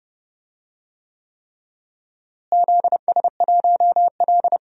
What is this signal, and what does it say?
Morse code, heard as a keyed tone, about 30 wpm, 700 Hz. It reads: ZS1L
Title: DXCC-100 Call Signs - 4 Characters - Repeated 3x - 30wpm